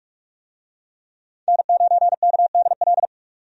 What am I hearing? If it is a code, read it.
Morse code, heard as a keyed tone, about 46 wpm, 700 Hz. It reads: N9KDL
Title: USA Call Signs - 5 Characters - 45wpm